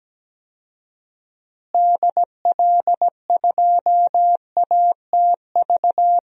Morse code, heard as a keyed tone, about 17 wpm, 700 Hz. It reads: DL2ATV
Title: DXCC-100 Call Signs - 6 Characters - Repeated 3x - 17wpm